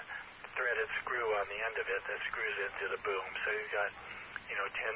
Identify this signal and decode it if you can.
Emergency Alert System (SAME) tones, heard as harsh buzzing bursts, with speech. It is heard from 0.5 to 4.5 s.